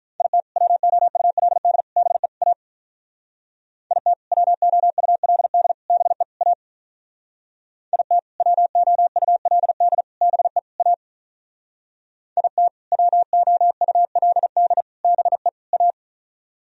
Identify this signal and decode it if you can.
Morse code, heard as a keyed tone, about 44 wpm, 700 Hz. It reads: IT WOULD BE A IT WOULD BE A IT WOULD BE A IT WOULD BE A